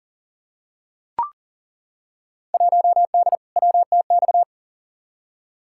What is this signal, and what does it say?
Morse code, heard as a keyed tone, about 40 wpm, 700 Hz. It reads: 1D WTX
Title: Field Day Exchanges - 40wpm